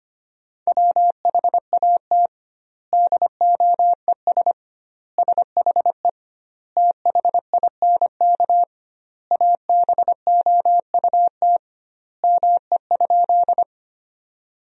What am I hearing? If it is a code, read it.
Morse code, heard as a keyed tone, about 25 wpm, 700 Hz. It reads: WHAT DOES SHE THINK ABOUT ME?
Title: 2x Word-Spacing - Sentences from Top 200 Words - 25wpm